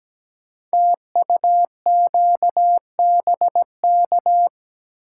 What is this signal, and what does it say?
Morse code, heard as a keyed tone, about 17 wpm, 700 Hz. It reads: TUQBK
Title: Code Groups - 5 Letters - 17wpm